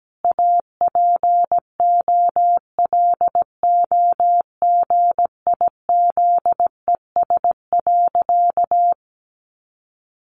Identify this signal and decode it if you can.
Morse code, heard as a keyed tone, about 17 wpm, 700 Hz. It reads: APOLOGIZES.
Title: Additional Easy Phrases 17wpm